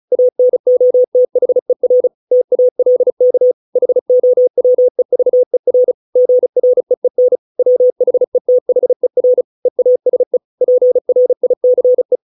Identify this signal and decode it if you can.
Morse code, heard as a keyed tone, about 34 wpm, 500 Hz. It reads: ANOTHER TALK HOWEVER GREEN WHETHER EASE PRICE